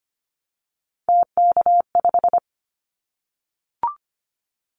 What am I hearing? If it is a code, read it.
Morse code, heard as a keyed tone, about 25 wpm, 700 Hz. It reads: TK5
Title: Callsign Prefixes for Top 100 DXCC - 25wpm